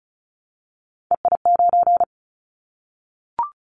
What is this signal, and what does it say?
Morse code, heard as a keyed tone, about 36 wpm, 690 Hz. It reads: EI9
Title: Callsign Prefixes for Top 50 DXCC - 35wpm